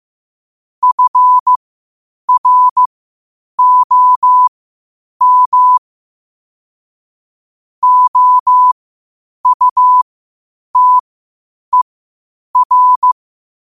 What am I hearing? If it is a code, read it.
Morse code, heard as a keyed tone, about 15 wpm, spaced as for 8 wpm, 1000 Hz. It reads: FROM OUTER
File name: qotdfiles / 8wpm